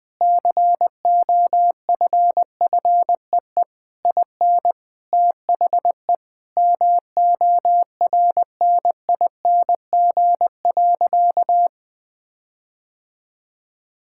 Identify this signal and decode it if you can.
Morse code, heard as a keyed tone, about 20 wpm, 700 Hz. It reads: COFFEE IN THE MORNING.
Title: Difficult Sentences with Hints - 20wpm